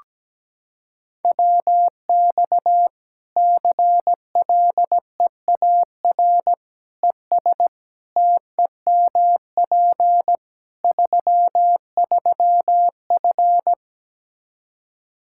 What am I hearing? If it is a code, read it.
Morse code, heard as a keyed tone, about 17 wpm, 700 Hz. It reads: WX CLEAR ES TEMP 33F
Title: Practice QSO - Exchange 2 - Info - 17wpm